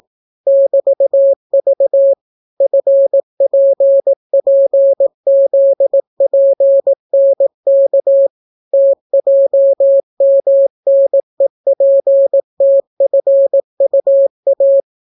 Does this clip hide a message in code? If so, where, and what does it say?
Yes: Morse code from 0.5 to 14.8 s, heard as a keyed tone, about 18 wpm, 550 Hz. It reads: <BT>V FPPZPNK TJMNEPTFUA